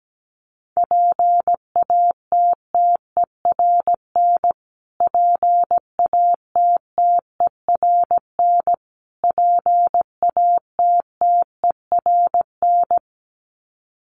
Morse code, heard as a keyed tone, about 17 wpm, 700 Hz. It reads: PATTERN PATTERN PATTERN
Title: Top 500 Words Repeated 17wpm